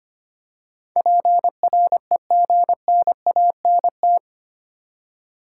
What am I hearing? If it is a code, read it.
Morse code, heard as a keyed tone, about 25 wpm, 700 Hz. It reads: PREGNANT
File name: Eight Letter Words - 25wpm